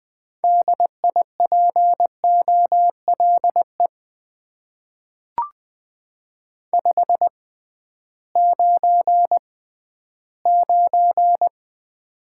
Morse code, heard as a keyed tone, about 20 wpm, spaced as for 13 wpm, 700 Hz. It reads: DIPOLE 599
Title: Top 90 QSO Elements - 6x Character Spacing - 20wpm